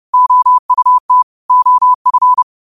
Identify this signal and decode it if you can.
Morse code, heard as a keyed tone, about 30 wpm, 1000 Hz. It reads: OUT OF
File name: qotdfiles / 30wpm